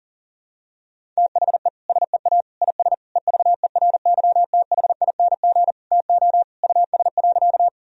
Morse code, heard as a keyed tone, about 40 wpm, 700 Hz. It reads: THE SEA IS EVERYTHING TO US.
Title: Sentences from Top 1000 Words - Review Part 1 - 40wpm